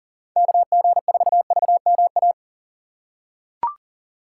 Morse code, heard as a keyed tone, about 40 wpm, 700 Hz. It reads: KG4VMA
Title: USA Call Signs - 6 Characters - 4x Character Spacing - 40wpm